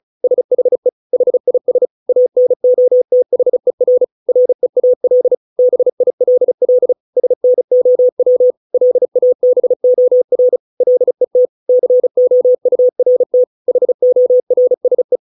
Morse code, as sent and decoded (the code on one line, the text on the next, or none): ... .... / .... .. ... / .- -. --- - .... .-. / .-. .- .-.. / -... .. .-.. .-.. / ... -. --- .-- / .-.. .- -... --- .-. / .-.. - / -.-. --- ..- .-. - / .... --- .-. ... .
SHE HIS ANOTHER REAL BILL SNOW LABOR LET COURT HORSE